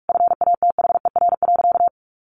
Morse code, heard as a keyed tone, about 44 wpm, 700 Hz. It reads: FATHER.
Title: Sentences from Top 1500 Words 45wpm